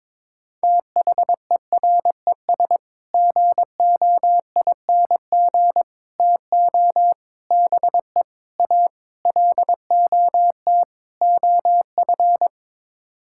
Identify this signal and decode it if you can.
Morse code, heard as a keyed tone, about 22 wpm, 700 Hz. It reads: THERES GOING TO BE A LOT OF
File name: practice / 22wpm / Sets of 7 Words - 22wpm